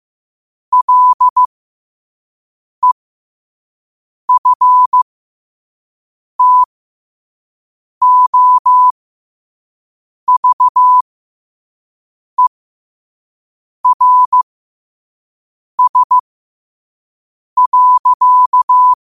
Morse code, heard as a keyed tone, about 15 wpm, spaced as for 5 wpm, 1000 Hz. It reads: LEFTOVERS.
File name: qotdfiles / 5wpm